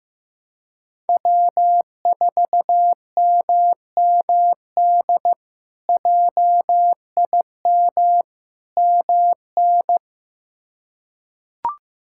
Morse code, heard as a keyed tone, about 15 wpm, 700 Hz. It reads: W4MMD JIM MN
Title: K1USN SST QSO Practice - 15wpm